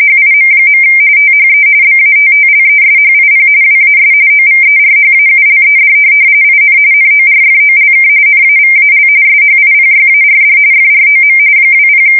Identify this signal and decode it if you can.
RTTY, about 45.45 baud. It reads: HIM. THE BATTLESHIP WAS OVERHAULED IN NORFOLK NAVAL SHIPYARD FROM 20
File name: fsk170 / USS-Missouri